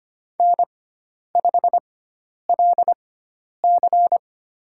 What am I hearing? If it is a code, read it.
Morse code, heard as a keyed tone, about 25 wpm, spaced as for 10 wpm, 700 Hz. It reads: N5LC
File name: Lesson 16g - Letter c - Call Signs 25wpm